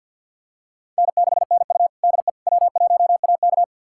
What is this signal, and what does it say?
Morse code, heard as a keyed tone, about 50 wpm, 700 Hz. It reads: N6NU DE W1AK
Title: USA Callsign DE Callsign - 4 Characters - 50wpm